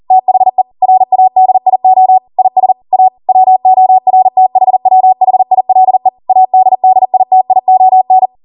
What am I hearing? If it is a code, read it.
Morse code, heard as a keyed tone, about 40 wpm, 750 Hz. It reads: THE RADIO IS A WORTHWHILE ADDITION